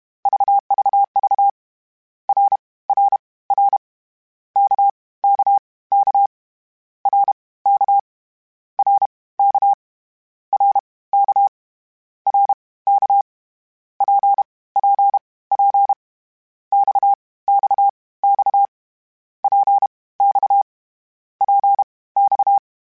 Morse code, heard as a keyed tone, about 32 wpm, spaced as for 19 wpm, 800 Hz. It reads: VVV RRR KKK RK RK RK RK PPP XXX PX PX
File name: Short Runs / Short Runs-mp3s / Balanced @30